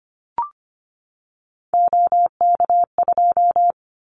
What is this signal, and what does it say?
Morse code, heard as a keyed tone, about 25 wpm, 700 Hz. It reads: OK2
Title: Callsign Prefixes for Top 50 DXCC - 25wpm